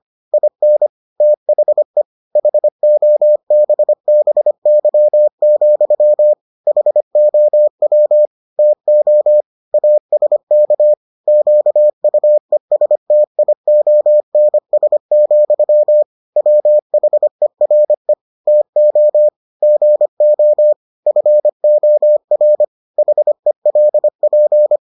Morse code, heard as a keyed tone, about 25 wpm, 600 Hz. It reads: IN THE HOBBY, HOW TO ASK QUESTIONS, WHERE TO GO FOR HELP